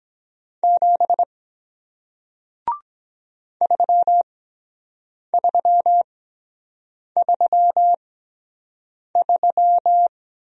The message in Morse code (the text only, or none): 7 3333